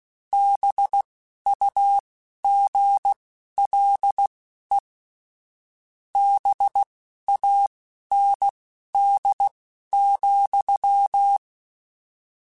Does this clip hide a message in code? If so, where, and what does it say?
Yes: Morse code from 0.3 to 11.4 s, heard as a keyed tone, about 16 wpm, spaced as for 12 wpm, 800 Hz. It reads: BUGLE BAND,